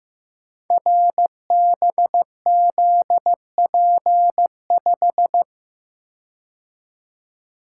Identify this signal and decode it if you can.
Morse code, heard as a keyed tone, about 15 wpm, 700 Hz. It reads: RBZP5
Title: Code Groups - 5 Letter:Numbers - 15wpm